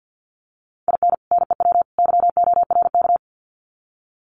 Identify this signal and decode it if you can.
Morse code, heard as a keyed tone, about 50 wpm, 700 Hz. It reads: IN NEW YORK
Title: Sets of 3 Words 50wpm